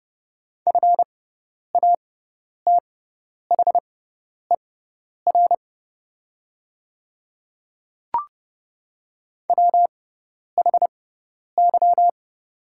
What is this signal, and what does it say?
Morse code, heard as a keyed tone, about 30 wpm, spaced as for 10 wpm, 700 Hz. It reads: FATHER WHY
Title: 6x Character Spacing - Top 300 Words - 30wpm